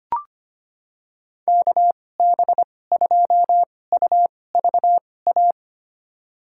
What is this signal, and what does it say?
Morse code, heard as a keyed tone, about 25 wpm, spaced as for 18 wpm, 700 Hz. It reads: KB2UVA
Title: USA Call Signs - 2x Character Spacing - 25wpm